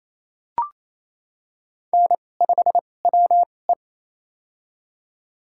Morse code, heard as a keyed tone, about 28 wpm, spaced as for 20 wpm, 700 Hz. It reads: N5WE
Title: USA Call Signs - 4 Characters - 2x Character Spacing - 28wpm